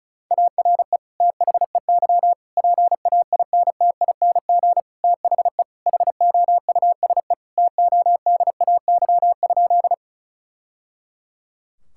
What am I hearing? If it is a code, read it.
Morse code, heard as a keyed tone, about 36 wpm, 700 Hz. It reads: ARE THEY PAINTING THE HOUSE TODAY?
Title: Sentences from Top 1000 Words - Review Part 6 - 35wpm